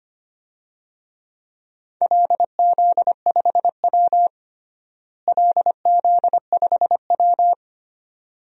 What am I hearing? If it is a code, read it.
Morse code, heard as a keyed tone, about 25 wpm, 700 Hz. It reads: LZ5W LZ5W
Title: DXCC-100 Call Signs - 4 Characters - Repeated 2x - 25wpm